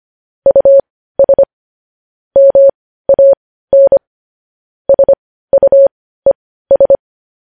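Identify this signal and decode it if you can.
Morse code, heard as a keyed tone, about 25 wpm, spaced as for 15 wpm, 550 Hz. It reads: US MAN SUES
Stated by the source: Morse code news headlines at 15 wpm.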